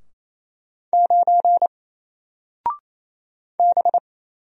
Morse code, heard as a keyed tone, about 27 wpm, 700 Hz. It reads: 9 B
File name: Single-Letter-Number-28wpm